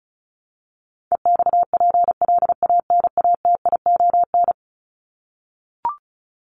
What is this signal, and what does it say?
Morse code, heard as a keyed tone, about 34 wpm, 700 Hz. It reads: EXPLANATION
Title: Top 1751-2000 Words 35wpm